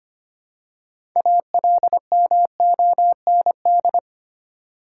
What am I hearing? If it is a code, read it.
Morse code, heard as a keyed tone, about 25 wpm, 700 Hz. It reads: ALMOND
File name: practice / 25wpm / Easy Words with Hints - 25wpm